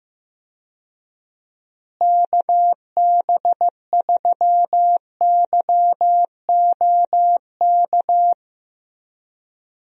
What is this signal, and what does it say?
Morse code, heard as a keyed tone, about 15 wpm, 700 Hz. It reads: KB3YOK
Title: USA Call Signs - 6 Characters - Repeated 3x - 15wpm